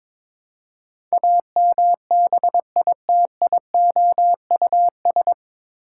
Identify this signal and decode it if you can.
Morse code, heard as a keyed tone, about 22 wpm, 700 Hz. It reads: AMBITIOUS
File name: 9 Letter Words - First 3 Letters Given - 22wpm